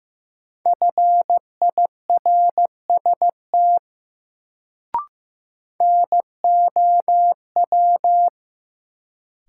Morse code, heard as a keyed tone, about 15 wpm, 700 Hz. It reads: FIRST NOW
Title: Sets of 2 Words - New Word Drill - 15wpm